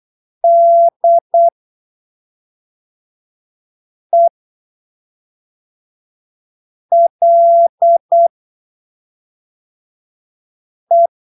Morse code, heard as a keyed tone, about 8 wpm, spaced as for 3 wpm, 680 Hz. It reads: DELE
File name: arquivos / texto / texto 17 08ppm